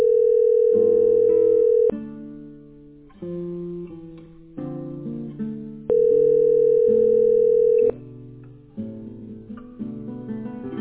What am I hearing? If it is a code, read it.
Ringback tone (North America), 440 + 480 Hz.